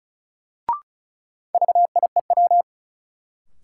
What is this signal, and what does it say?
Morse code, heard as a keyed tone, about 36 wpm, 700 Hz. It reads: VIEW
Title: Sets of 5 Words - New Word Drill - 35wpm